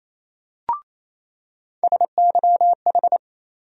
Morse code, heard as a keyed tone, about 28 wpm, 700 Hz. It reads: SYH
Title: USA Call Sign Postfixes - 3 Letters 28wpm